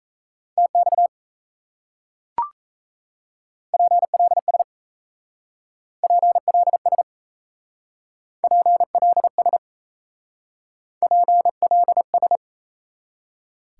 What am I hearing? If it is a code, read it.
Morse code, heard as a keyed tone, about 38 wpm, 700 Hz. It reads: TX PLS PLS PLS PLS